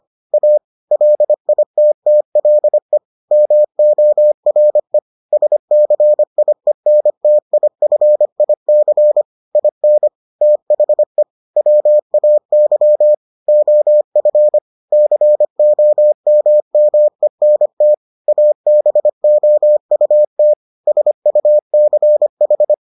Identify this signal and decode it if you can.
Morse code, heard as a keyed tone, about 25 wpm, 600 Hz. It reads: A LITTLE MORE SCIENTIFIC IN THE WAY OF COMMENT ABOUT SUCH